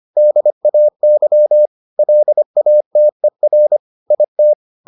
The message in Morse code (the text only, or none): DAY LATER IT